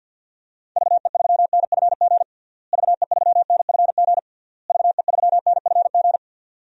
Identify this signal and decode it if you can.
Morse code, heard as a keyed tone, about 51 wpm, 700 Hz. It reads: VE3NFG VE3NFG VE3NFG
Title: DXCC-100 Call Signs - 6 Characters - Repeated 3x - 50wpm